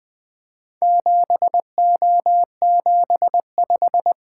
Morse code, heard as a keyed tone, about 20 wpm, 700 Hz. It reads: 7O75
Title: Four Letter-Number Combinations - 20wpm